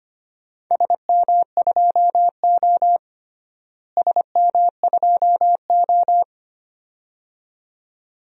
Morse code, heard as a keyed tone, about 25 wpm, 700 Hz. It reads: SM2O SM2O
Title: DXCC-100 Call Signs - 4 Characters - Repeated 2x - 25wpm